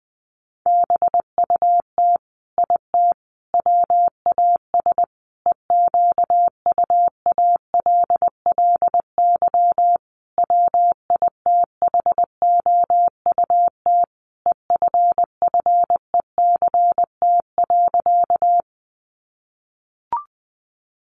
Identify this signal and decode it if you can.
Morse code, heard as a keyed tone, about 20 wpm, 700 Hz. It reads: BUT IT WAS EQUALLY WITHOUT EFFECT.
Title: Sentences from Top 1400 Words 20wpm